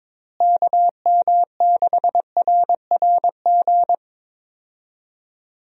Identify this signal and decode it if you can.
Morse code, heard as a keyed tone, about 21 wpm, 700 Hz. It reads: KM6RRG
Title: USA Call Signs - 6 Characters - Repeated 2x - 22wpm